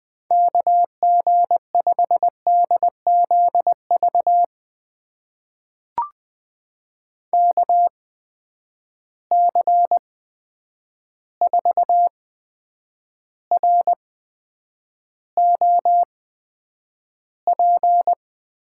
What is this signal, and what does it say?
Morse code, heard as a keyed tone, about 20 wpm, spaced as for 9 wpm, 700 Hz. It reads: KG5DZV KC4ROP
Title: USA Call Signs - 6 Characters - 8x Character Spacing - 20wpm